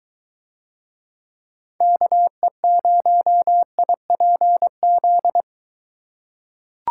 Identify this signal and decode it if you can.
Morse code, heard as a keyed tone, about 23 wpm, 700 Hz. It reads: KE0IPZ